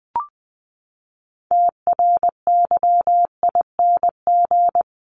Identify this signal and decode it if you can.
Morse code, heard as a keyed tone, about 20 wpm, 700 Hz. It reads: TRYING
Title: ING Suffix - Top 100 Words with Base Words 20wpm